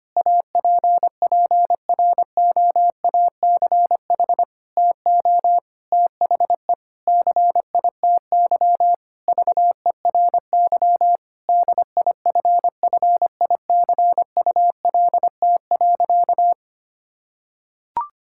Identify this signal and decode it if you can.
Morse code, heard as a keyed tone, about 25 wpm, 700 Hz. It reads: APPROACH TO THE CITY VERY DIFFICULT.
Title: Sentences from Top 1000 Words - Review Part 6 - 25wpm